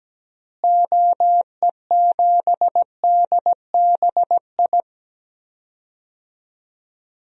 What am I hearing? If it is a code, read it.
Morse code, heard as a keyed tone, about 17 wpm, 700 Hz. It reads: OE7DBI
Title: DXCC-100 Call Signs - 6 Characters - Repeated 3x - 17wpm